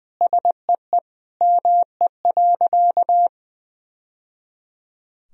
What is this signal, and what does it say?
Morse code, heard as a keyed tone, about 20 wpm, 700 Hz. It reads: SEE ME.
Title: Sentences from Top 1000 Words - Review Part 1 - 20wpm